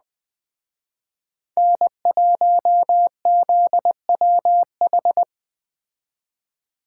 Morse code, heard as a keyed tone, about 20 wpm, 700 Hz. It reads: N1ZWH